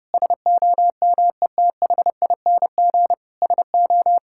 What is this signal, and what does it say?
Morse code, heard as a keyed tone, about 30 wpm, 700 Hz. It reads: SOMETHING SO
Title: Sentences from Top 1000 Words - Review Part 6 - 30wpm